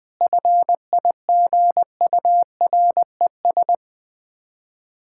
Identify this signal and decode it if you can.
Morse code, heard as a keyed tone, about 20 wpm, 690 Hz. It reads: FIGURES
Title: Difficult Words with Hints - 20wpm